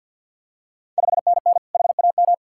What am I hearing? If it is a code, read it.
Morse code, heard as a keyed tone, about 51 wpm, 690 Hz. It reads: 5NN HAM
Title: Ontario QSO Party Exchange 50wpm